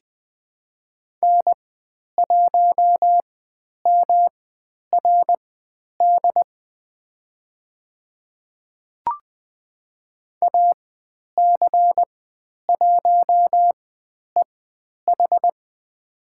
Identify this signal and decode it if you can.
Morse code, heard as a keyed tone, about 20 wpm, spaced as for 10 wpm, 700 Hz. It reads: N1MRD AC1EH
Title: 109 Lesson 18g - Letter m - Call Signs 20wpm